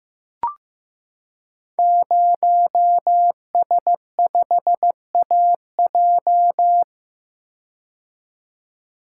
Morse code, heard as a keyed tone, about 16 wpm, 700 Hz. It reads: 0S5AJ
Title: Code Groups - 5 Letter:Numbers - 15wpm